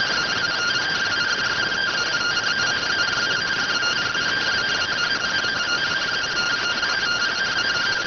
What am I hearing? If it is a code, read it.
Two-tone teleprinter signal, about 170 Hz shift.
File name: AXIAL fr130 3mm 409.gen-05 rx